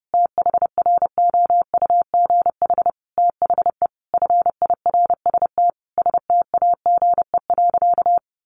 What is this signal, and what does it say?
Morse code, heard as a keyed tone, about 28 wpm, 700 Hz. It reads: THROUGH THE FIRST STAGE.
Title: Sentences from Top 1200 Words 30wpm